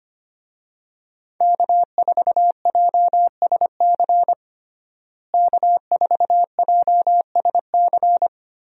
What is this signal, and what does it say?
Morse code, heard as a keyed tone, about 25 wpm, 700 Hz. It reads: K4JSC K4JSC
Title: USA Call Signs - 5 Characters - Repeated 3x - 25wpm